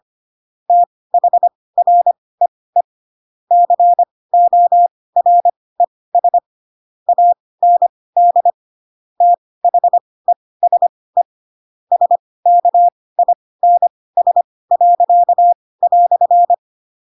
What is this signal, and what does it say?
Morse code, heard as a keyed tone, about 25 wpm, spaced as for 18 wpm, 700 Hz. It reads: THREE CORES AND THESE SKINS."
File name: Pinocchio / Pinocchio ch7 12 25 ik1hge